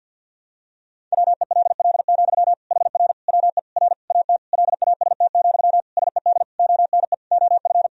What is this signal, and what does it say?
Morse code, heard as a keyed tone, about 52 wpm, 700 Hz. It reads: WELL, HR WE R AT LAST, SED ONE OF
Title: Sentences from Top 300 Words and Abbreviations 50wpm